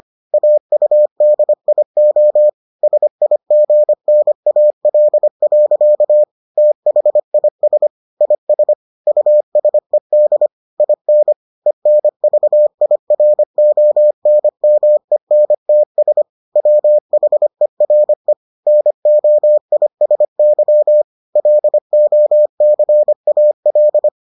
Morse code, heard as a keyed tone, about 25 wpm, 600 Hz. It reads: AUDIO SIGNAL. THIS IS USED IN ENVIRONMENTS WHERE NOISY LOCAL